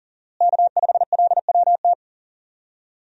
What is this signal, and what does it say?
Morse code, heard as a keyed tone, about 40 wpm, 700 Hz. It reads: K5LWT
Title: USA Call Signs - 5 Characters - Repeated 3x - 40wpm